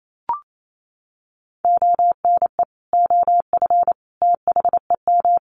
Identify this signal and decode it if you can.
Morse code, heard as a keyed tone, about 28 wpm, 700 Hz. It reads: ONE OF THEM
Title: Sets of 3 Words 28wpm